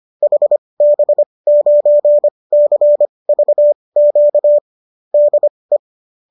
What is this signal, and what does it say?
Morse code, heard as a keyed tone, about 25 wpm, spaced as for 20 wpm, 590 Hz. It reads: HB9CVQ DE